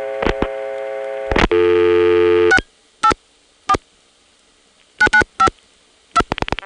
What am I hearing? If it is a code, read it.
DTMF, heard as telephone touch tones, tones of about 80 ms. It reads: 684 9952